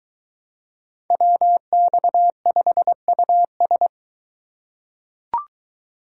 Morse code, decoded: WX5US

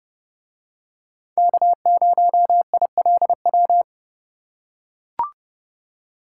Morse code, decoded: K0ILW

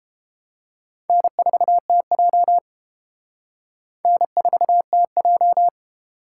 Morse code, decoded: N4TJ N4TJ

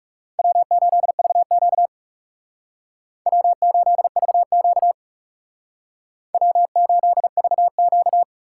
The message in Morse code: W8VQ W8VQ W8VQ

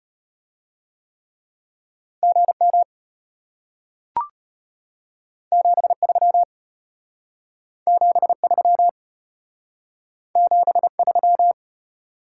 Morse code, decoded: GM 73 73 73

